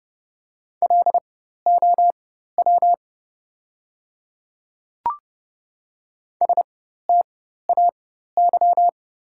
Morse code, heard as a keyed tone, about 30 wpm, spaced as for 14 wpm, 700 Hz. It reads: LOW STAY
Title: 4x Character Spacing - Top 500 Words - 30wpm